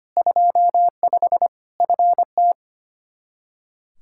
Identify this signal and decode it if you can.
Morse code, heard as a keyed tone, about 25 wpm, 700 Hz. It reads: 25 FT
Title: K1USN SST QSO Practice 25wpm